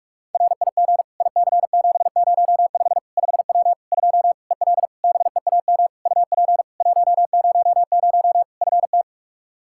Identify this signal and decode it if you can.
Morse code, heard as a keyed tone, about 42 wpm, 700 Hz. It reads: RIG IC705 5W 2 EL BEAM UP 100 FT
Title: K1USN SST QSO Practice 45wpm